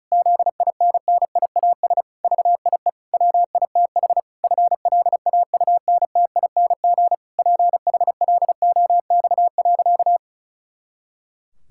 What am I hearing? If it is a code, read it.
Morse code, heard as a keyed tone, about 36 wpm, 700 Hz. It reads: ZINNIAS VIE WITH FLAUNTING PHLOX.
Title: Pangram Challenge v2 - 35wpm